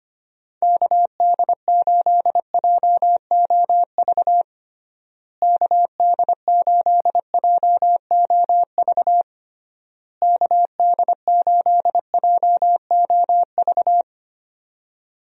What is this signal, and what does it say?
Morse code, heard as a keyed tone, about 25 wpm, 700 Hz. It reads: KD8JOV KD8JOV KD8JOV
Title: USA Call Signs - 6 Characters - Repeated 3x - 25wpm